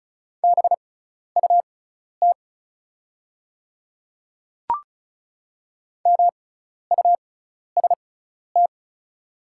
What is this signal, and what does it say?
Morse code, heard as a keyed tone, about 36 wpm, spaced as for 12 wpm, 700 Hz. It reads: BUT MUST
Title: 6x Character Spacing - Top 200 Words - 35wpm